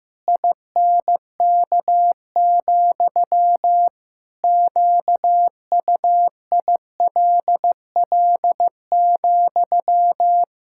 Morse code, heard as a keyed tone, about 15 wpm, 700 Hz. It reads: INK, QUILL,
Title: Pangram Challenge v2 - 15wpm